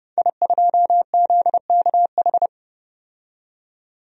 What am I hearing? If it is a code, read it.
Morse code, heard as a keyed tone, about 32 wpm, 700 Hz. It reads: I2ZKH
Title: DXCC-100 Call Signs - 5 Characters - 30wpm